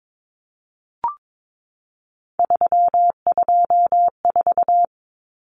Morse code, heard as a keyed tone, about 22 wpm, 700 Hz. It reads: 324
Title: Numbers - 3-Digit 22wpm